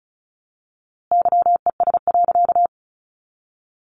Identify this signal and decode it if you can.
Morse code, heard as a keyed tone, about 34 wpm, 700 Hz. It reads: YES.
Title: A Sound Of Thunder 35wpm